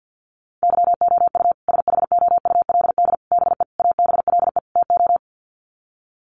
Morse code, heard as a keyed tone, about 50 wpm, 700 Hz. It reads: YOU SHOULD BE ABLE TO
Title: Sets of 5 Words Encore 50wpm